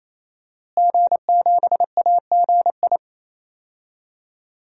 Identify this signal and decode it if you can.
Morse code, heard as a keyed tone, about 28 wpm, 700 Hz. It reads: G7AGI
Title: DXCC-100 Call Signs - 5 Characters - Repeated 3x - 28wpm